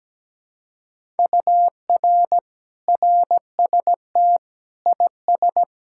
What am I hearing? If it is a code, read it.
Morse code, heard as a keyed tone, about 17 wpm, 700 Hz. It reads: UR RST IS
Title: Speed-Racing Common QSO Phrases - 17wpm